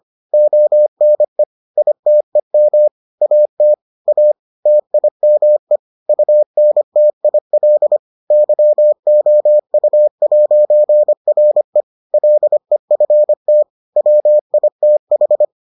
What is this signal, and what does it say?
Morse code, heard as a keyed tone, about 25 wpm, 600 Hz. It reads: ONE ITEM AT A TIME UNTIL YOU'RE LEFT WITH